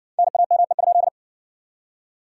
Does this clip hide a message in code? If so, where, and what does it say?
Yes: Morse code from 0.2 to 1.1 s, heard as a keyed tone, about 57 wpm, 700 Hz. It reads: NAME?